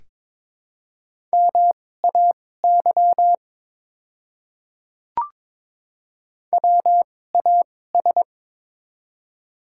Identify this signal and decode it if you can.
Morse code, heard as a keyed tone, about 22 wpm, spaced as for 16 wpm, 700 Hz. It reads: MAY WAS
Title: 2x Character Spacing - Top 100 Words - 22wpm